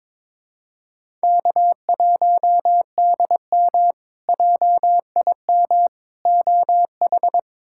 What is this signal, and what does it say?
Morse code, heard as a keyed tone, about 22 wpm, 700 Hz. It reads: K1DM JIM OH
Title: K1USN SST QSO Practice - 22wpm